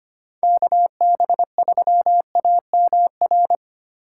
Morse code, heard as a keyed tone, about 33 wpm, 700 Hz. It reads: KB3AMR